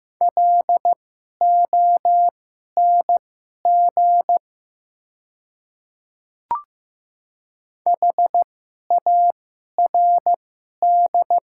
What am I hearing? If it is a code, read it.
Morse code, heard as a keyed tone, about 15 wpm, spaced as for 11 wpm, 700 Hz. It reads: LONG HARD
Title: 2x Character Spacing - Top 300 Words - 15wpm